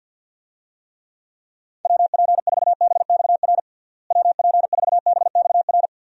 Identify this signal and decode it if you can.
Morse code, heard as a keyed tone, about 52 wpm, 700 Hz. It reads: WP4BXR WP4BXR
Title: USA Call Signs - 6 Characters - Repeated 2x - 50wpm